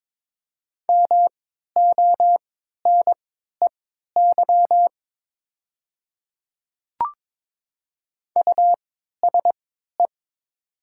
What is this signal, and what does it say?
Morse code, heard as a keyed tone, about 22 wpm, spaced as for 12 wpm, 700 Hz. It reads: MONEY USE
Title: 3x Character Spacing - Top 500 Words - 22wpm